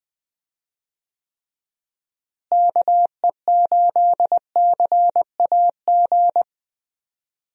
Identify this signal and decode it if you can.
Morse code, heard as a keyed tone, about 20 wpm, 700 Hz. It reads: KE8CAG
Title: USA Call Signs - 6 Characters - Repeated 2x - 20wpm